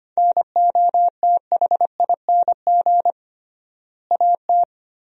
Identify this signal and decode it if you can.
Morse code, heard as a keyed tone, about 25 wpm, 700 Hz. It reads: NOTHING AT
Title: 3x Word-Spacing - Sentences from Top 500 Words - 25wpm